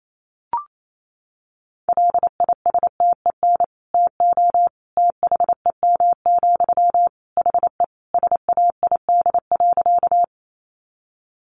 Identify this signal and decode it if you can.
Morse code, heard as a keyed tone, about 28 wpm, 700 Hz. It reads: LISTEN TO THEM, HE SAID.
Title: Sentences from Top 400 Words 28wpm